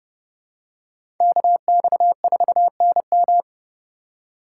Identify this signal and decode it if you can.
Morse code, heard as a keyed tone, about 29 wpm, 700 Hz. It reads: KX4NM
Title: K1USN SST QSO Practice - 30wpm